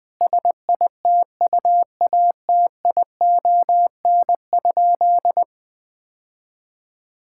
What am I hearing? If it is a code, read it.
Morse code, heard as a keyed tone, about 21 wpm, 700 Hz. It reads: SITUATION?
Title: Sentences with 8-Word Phrases - 20wpm